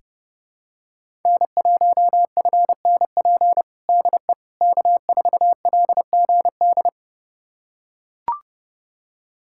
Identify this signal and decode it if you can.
Morse code, heard as a keyed tone, about 32 wpm, 700 Hz. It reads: N1FNP DE K4LGD
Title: USA Callsign DE Callsign - 5 Characters - 30wpm